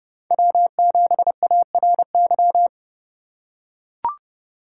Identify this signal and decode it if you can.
Morse code, heard as a keyed tone, about 30 wpm, 700 Hz. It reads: W7ARY
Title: USA Call Signs - 5 Characters - 3x Character Spacing - 30wpm